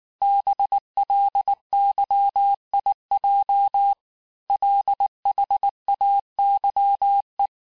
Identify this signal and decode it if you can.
Morse code, heard as a keyed tone, about 19 wpm, 800 Hz. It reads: BLYIJ LHAYE
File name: cours-du-lundi files / cours-1589-texte-chiffre-vitesse-18mpm